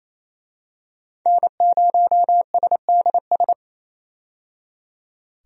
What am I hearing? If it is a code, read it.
Morse code, heard as a keyed tone, about 28 wpm, 700 Hz. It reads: N0SDS
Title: USA Call Signs - 5 Characters - 28wpm